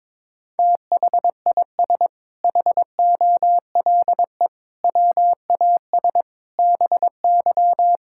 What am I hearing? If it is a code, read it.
Morse code, heard as a keyed tone, about 22 wpm, 700 Hz. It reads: THIS HOLE WAS BY
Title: Sentences from Top 1600 Words 22wpm